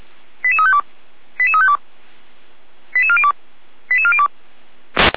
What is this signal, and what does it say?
Five-tone selective call (ZVEI), heard as quick runs of five tones, tones of about 70 ms. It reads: 88352 88352 88402 88402